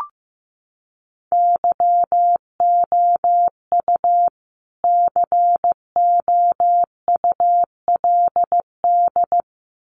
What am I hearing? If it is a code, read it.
Morse code, heard as a keyed tone, about 14 wpm, 700 Hz. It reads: YOU COULD
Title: Sets of 2 Words 15wpm